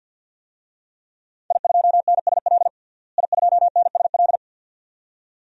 Morse code, heard as a keyed tone, about 53 wpm, 700 Hz. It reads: I2NSL I2NSL
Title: DXCC-100 Call Signs - 5 Characters - Repeated 2x - 50wpm